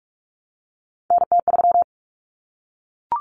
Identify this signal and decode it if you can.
Morse code, heard as a keyed tone, about 44 wpm, 700 Hz. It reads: NT3
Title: Common Cut Numbers - 3-Digit 45wpm